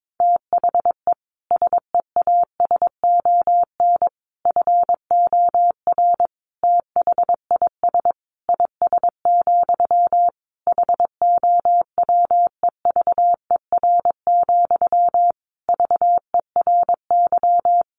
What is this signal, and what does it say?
Morse code, heard as a keyed tone, about 22 wpm, 700 Hz. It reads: THE SEASON FOR THIS IS, HOWEVER, VERY
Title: Sentences from Top 1400 Words 22wpm